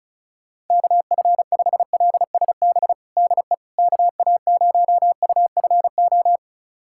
Morse code, heard as a keyed tone, about 34 wpm, 690 Hz. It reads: KF5LSB DE KA0UFO
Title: USA Callsign DE Callsign 35wpm